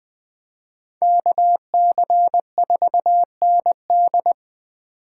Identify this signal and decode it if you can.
Morse code, heard as a keyed tone, about 20 wpm, 700 Hz. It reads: KC4ND
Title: Speed Racing - USA 2x2 Call Signs 17wpm